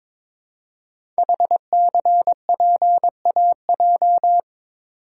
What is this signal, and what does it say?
Morse code, heard as a keyed tone, about 22 wpm, 700 Hz. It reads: HCPAJ
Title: Code Groups - 5 Letters - 22wpm